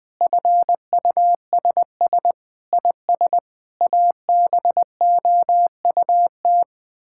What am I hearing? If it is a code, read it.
Morse code, heard as a keyed tone, about 21 wpm, 700 Hz. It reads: FUSS IS ABOUT
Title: Sets of 8 Words - 20wpm